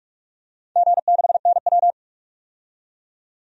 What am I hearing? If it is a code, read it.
Morse code, heard as a keyed tone, about 48 wpm, 700 Hz. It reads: G6NW
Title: DXCC-100 Call Signs - 4 Characters - 45wpm